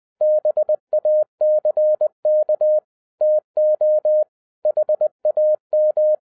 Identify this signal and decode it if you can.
Morse code, heard as a keyed tone, about 20 wpm, 610 Hz. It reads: BACK TO HAM
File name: 2012-01-mp3 / n05-CAME BACK TO HAM RADIO AFTER 30 YEARS OF INTERVAL-20wpm